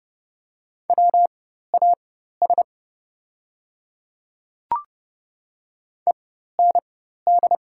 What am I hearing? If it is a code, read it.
Morse code, heard as a keyed tone, about 30 wpm, spaced as for 14 wpm, 700 Hz. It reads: WAS END
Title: 4x Character Spacing - Top 200 Words - 30wpm